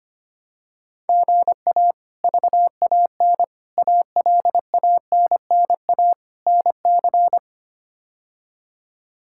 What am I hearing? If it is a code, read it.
Morse code, heard as a keyed tone, about 25 wpm, 700 Hz. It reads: GA VAN ALANNA NC
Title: K1USN SST QSO Practice - 25wpm